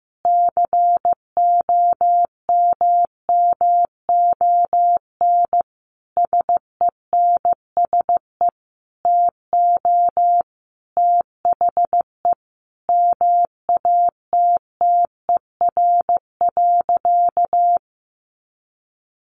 Morse code, heard as a keyed tone, about 15 wpm, 700 Hz. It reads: COMMON SENSE TO THE MATTER.
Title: Sentences from Top 1200 Words 15wpm